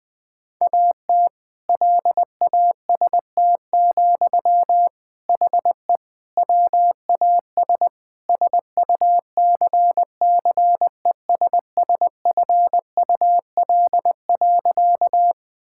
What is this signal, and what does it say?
Morse code, heard as a keyed tone, about 21 wpm, 700 Hz. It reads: AT LAST, HE WAS SUCCESSFUL.
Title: Sentences from Top 1000 Words - Review Part 6 - 20wpm